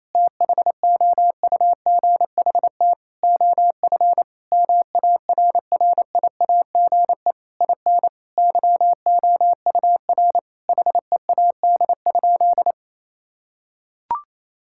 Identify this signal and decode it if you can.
Morse code, heard as a keyed tone, about 28 wpm, 700 Hz. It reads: THOUGHT OF MARRIAGE IN YOUR HEAD?
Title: Sentences from Top 1000 Words - Review Part 5 - 28wpm